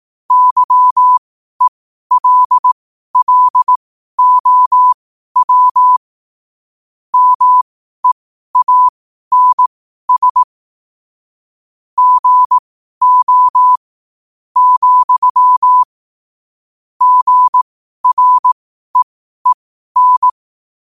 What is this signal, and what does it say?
Morse code, heard as a keyed tone, about 18 wpm, spaced as for 12 wpm, 1000 Hz. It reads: YELLOW MEANS GO, GREEN
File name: qotdfiles / 12wpm